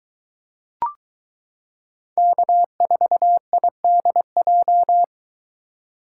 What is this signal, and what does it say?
Morse code, heard as a keyed tone, about 22 wpm, 700 Hz. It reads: K4IDJ